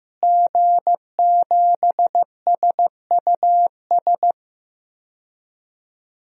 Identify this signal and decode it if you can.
Morse code, heard as a keyed tone, about 16 wpm, 700 Hz. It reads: G7SUS
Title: DXCC-100 Call Signs - 5 Characters - Repeated 3x - 15wpm